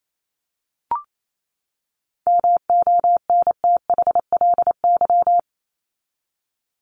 Morse code, heard as a keyed tone, about 28 wpm, 700 Hz. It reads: MONTHLY
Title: LY Suffix - Top 100 Words with Base Words 28wpm